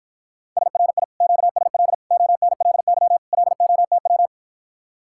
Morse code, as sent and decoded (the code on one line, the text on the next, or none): ... .-. .. / --.- ... .-.. / --- -. .-.. -.-- / .-.. --- - .--
SRI QSL ONLY LOTW